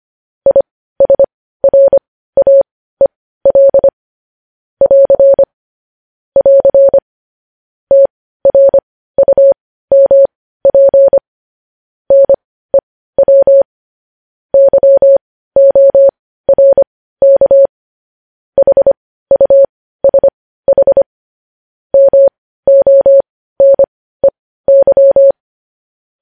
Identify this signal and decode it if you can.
Morse code, heard as a keyed tone, about 25 wpm, spaced as for 15 wpm, 550 Hz. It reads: ISRAEL <AR> <AR> TRUMP NEW YORK HUSH MONEY
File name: podcast / News headlines-1x-25x15